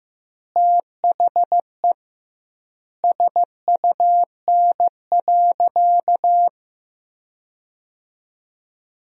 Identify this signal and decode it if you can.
Morse code, heard as a keyed tone, about 16 wpm, 700 Hz. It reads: THE SUN.
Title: 2x Word-Spacing - Sentences from Top 300 Words - 15wpm